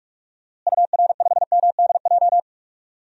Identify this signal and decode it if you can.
Morse code, heard as a keyed tone, about 46 wpm, 700 Hz. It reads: UR5MDJ
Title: DXCC-100 Call Signs - 6 Characters - Repeated 3x - 45wpm